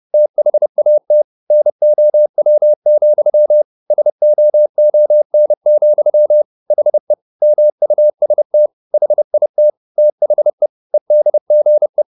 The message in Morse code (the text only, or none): THAT NOW, SOON, HE MUST HIT THE EDGE